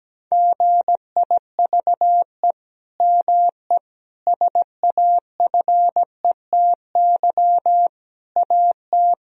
Morse code, heard as a keyed tone, about 17 wpm, 700 Hz. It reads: GIVE ME SAFETY AT
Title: Sentences from Top 1000 Words - Review Part 4 - 17wpm